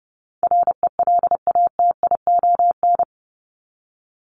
Morse code, heard as a keyed tone, about 30 wpm, 700 Hz. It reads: RELATION
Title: Top 1751-2000 Words 30wpm